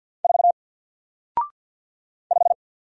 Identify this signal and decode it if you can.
Morse code, heard as a keyed tone, about 48 wpm, 700 Hz. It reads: 4 5
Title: Single Letter-Number Alt 50wpm